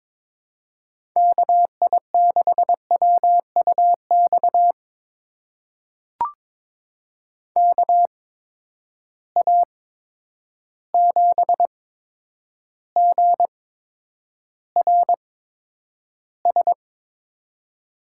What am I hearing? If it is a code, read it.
Morse code, heard as a keyed tone, about 22 wpm, spaced as for 9 wpm, 700 Hz. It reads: KI6WUX KA7GRS
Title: USA Call Signs - 6 Characters - 8x Character Spacing - 22wpm